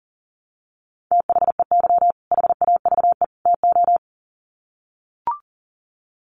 Morse code, heard as a keyed tone, about 40 wpm, 700 Hz. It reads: THEY HAVE TO
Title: Sets of 3 Words 40wpm